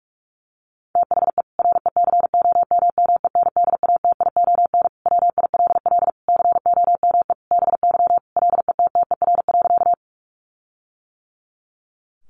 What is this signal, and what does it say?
Morse code, heard as a keyed tone, about 44 wpm, 700 Hz. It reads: THE RECOMMENDATION WILL COME BY LETTER.
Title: Sentences from Top 1400 Words 45wpm